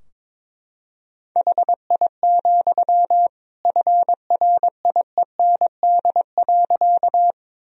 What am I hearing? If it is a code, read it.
Morse code, heard as a keyed tone, about 22 wpm, 700 Hz. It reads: HI, FRIEND.
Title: Sentences from Top 1000 Words - Review Part 3 - 22wpm